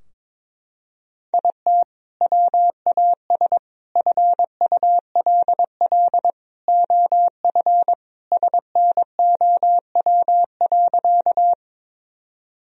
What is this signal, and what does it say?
Morse code, heard as a keyed tone, about 22 wpm, 700 Hz. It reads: IT WAS FULL OF SNOW.
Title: Sentences from Top 1000 Words - Review Part 3 - 22wpm